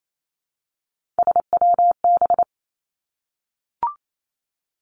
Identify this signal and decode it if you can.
Morse code, heard as a keyed tone, about 28 wpm, 700 Hz. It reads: SWB
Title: Three Challenging Letter-Number Combinations 28wpm